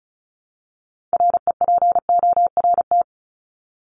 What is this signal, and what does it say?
Morse code, heard as a keyed tone, about 34 wpm, 700 Hz. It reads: REPORT
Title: Top 751-1000 Words 35wpm